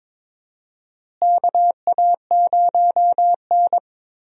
Morse code, heard as a keyed tone, about 22 wpm, 700 Hz. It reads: KA0N
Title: USA Call Signs - 4 Characters - Repeated 3x - 22wpm